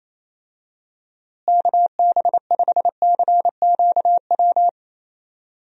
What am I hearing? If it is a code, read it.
Morse code, heard as a keyed tone, about 28 wpm, 700 Hz. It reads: KB5CQW
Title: USA Call Signs - 6 Characters - Repeated 2x - 28wpm